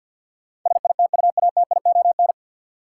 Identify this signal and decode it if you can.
Morse code, heard as a keyed tone, about 52 wpm, 700 Hz. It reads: SITUATION
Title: Sentences from Top 1200 Words - New Word Drill - 50wpm